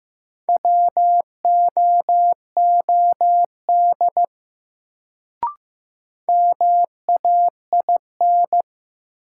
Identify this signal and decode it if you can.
Morse code, heard as a keyed tone, about 15 wpm, 700 Hz. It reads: WOOD MAIN
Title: Sentences from Top 1000 Words - Review Part 2 - New Word Drill - 15wpm